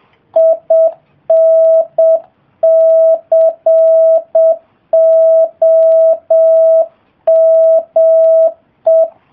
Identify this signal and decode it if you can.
Morse code, heard as a keyed tone, about 7 wpm, 660 Hz. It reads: INCOME